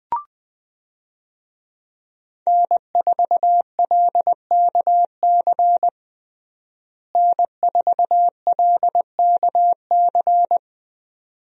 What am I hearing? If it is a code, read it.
Morse code, heard as a keyed tone, about 21 wpm, 700 Hz. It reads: N4LKC N4LKC
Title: USA Call Signs - 5 Characters - Repeated 2x - 20wpm